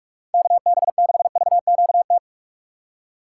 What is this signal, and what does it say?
Morse code, heard as a keyed tone, about 46 wpm, 700 Hz. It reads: KB6VQT
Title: USA Call Signs - 6 Characters - 6x Character Spacing - 45wpm